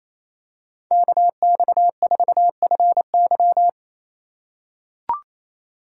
Morse code, heard as a keyed tone, about 28 wpm, 700 Hz. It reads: KX4FY
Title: USA Call Signs - 4x Character Spacing - 28wpm